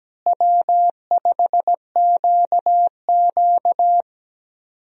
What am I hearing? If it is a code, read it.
Morse code, heard as a keyed tone, about 17 wpm, 700 Hz. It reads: W5QQ
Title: USA Call Signs - 4 Characters - Repeated 3x - 17wpm